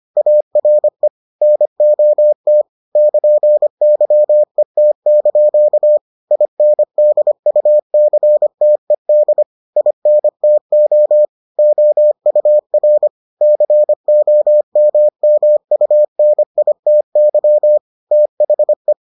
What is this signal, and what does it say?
Morse code, heard as a keyed tone, about 25 wpm, 600 Hz. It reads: ARE NOT <KN>YET) INDUCTED INTO OUR COMMUNITY THE